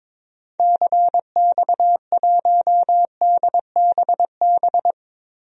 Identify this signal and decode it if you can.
Morse code, heard as a keyed tone, about 22 wpm, 700 Hz. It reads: CX1DBB
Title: DXCC-100 Call Signs - 6 Characters - Repeated 2x - 22wpm